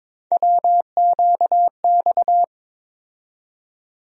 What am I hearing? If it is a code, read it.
Morse code, heard as a keyed tone, about 22 wpm, 700 Hz. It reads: WQX
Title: USA Call Sign Postfixes - 3 Letters 22wpm